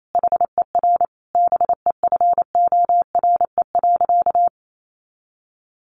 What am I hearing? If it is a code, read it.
Morse code, heard as a keyed tone, about 28 wpm, 700 Hz. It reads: HER BEFORE.
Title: Sentences from Top 600 Words 28wpm